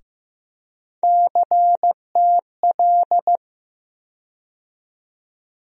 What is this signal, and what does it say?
Morse code, heard as a keyed tone, about 15 wpm, 700 Hz. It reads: CTL
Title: USA Call Sign Postfixes - 3 Letters 15wpm